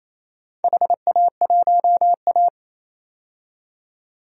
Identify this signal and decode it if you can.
Morse code, heard as a keyed tone, about 28 wpm, 700 Hz. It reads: HA1A
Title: DXCC-100 Call Signs - 4 Characters - 28wpm